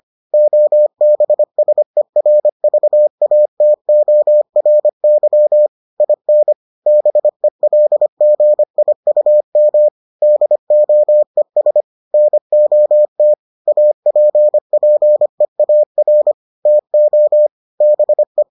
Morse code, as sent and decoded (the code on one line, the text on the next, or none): --- -... ... .-. ...- .- - --- .-. -.-- / .. -. / -... .-.. --. .. ..- -- / -.. --- . ... / -. --- - / .- .--. .--. .- .-. / - --- / -... .
OBSERVATORY IN BELGIUM DOES NOT APPEAR TO BE